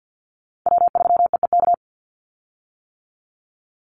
Morse code, heard as a keyed tone, about 50 wpm, 700 Hz. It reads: W3EEK
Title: Call Signs - 50wpm